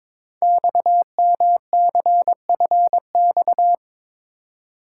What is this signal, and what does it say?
Morse code, heard as a keyed tone, about 22 wpm, 700 Hz. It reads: XMCFX
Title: Code Groups - 5 Letters - 22wpm